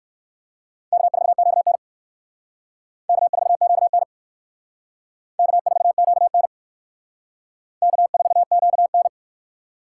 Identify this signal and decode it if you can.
Morse code, heard as a keyed tone, about 56 wpm, 700 Hz. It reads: K4QN K4QN K4QN K4QN
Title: Speed Racing - USA Call Signs 45wpm